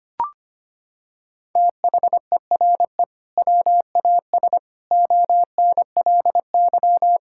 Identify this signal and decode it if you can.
Morse code, heard as a keyed tone, about 25 wpm, 700 Hz. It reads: THERE WAS ONLY
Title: Sentences from Top 1600 Words 25wpm